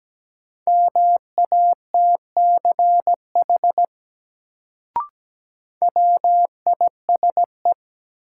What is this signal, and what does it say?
Morse code, heard as a keyed tone, about 17 wpm, 700 Hz. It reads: MATCH WISE
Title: Sentences from Top 1600 Words - New Word Drill - 17wpm